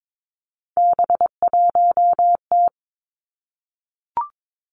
Morse code, heard as a keyed tone, about 22 wpm, 700 Hz. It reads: B1T
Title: Three Challenging Letter-Number Combinations 22wpm